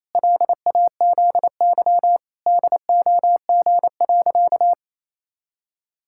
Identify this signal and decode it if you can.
Morse code, heard as a keyed tone, about 27 wpm, 700 Hz. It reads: LAZY DOG.
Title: Pangram Challenge v2 - 28wpm